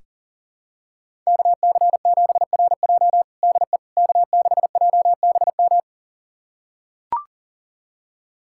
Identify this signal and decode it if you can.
Morse code, heard as a keyed tone, about 40 wpm, 700 Hz. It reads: KC7RJ DE K6JBM
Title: USA Callsign DE Callsign - 5 Characters - 40wpm